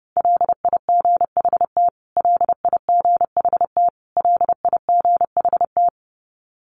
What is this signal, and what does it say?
Morse code, heard as a keyed tone, about 30 wpm, 700 Hz. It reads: LIGHT LIGHT LIGHT
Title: Top 300 Words Repeated 30wpm